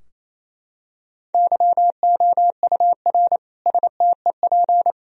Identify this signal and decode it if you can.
Morse code, heard as a keyed tone, about 28 wpm, 700 Hz. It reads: YOUR STEP
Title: Sentences from Top 1000 Words - Review Part 2 - 28wpm